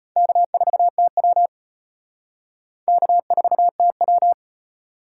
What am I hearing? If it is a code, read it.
Morse code, heard as a keyed tone, about 32 wpm, 700 Hz. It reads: K4TW K4TW